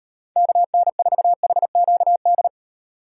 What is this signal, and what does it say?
Morse code, heard as a keyed tone, about 38 wpm, 700 Hz. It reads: KN4HQD